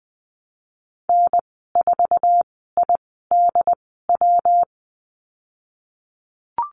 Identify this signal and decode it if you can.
Morse code, heard as a keyed tone, about 20 wpm, spaced as for 15 wpm, 700 Hz. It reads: N4IDW
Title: 167 Lesson 27g - Number 7 - Call Signs 20wpm